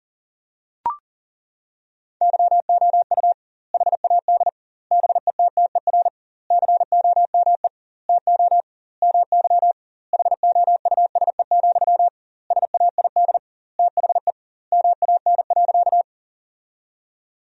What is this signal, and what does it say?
Morse code, heard as a keyed tone, about 40 wpm, 700 Hz. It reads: YOU HAD BETTER COME TO MY HOUSE, SAID THE MAN.
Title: 2x Word-Spacing - Sentences from Top 400 Words - 40wpm